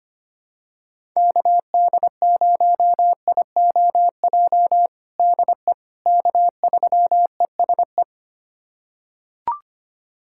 Morse code, heard as a keyed tone, about 25 wpm, 700 Hz. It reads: KD0IOJ DE K3ESE
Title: USA Callsign DE Callsign 25wpm